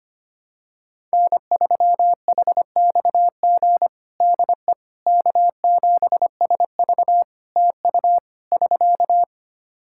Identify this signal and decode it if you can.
Morse code, heard as a keyed tone, about 25 wpm, 700 Hz. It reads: N3HXG DE K7SV TU <SK>
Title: K1USN SST QSO Practice 25wpm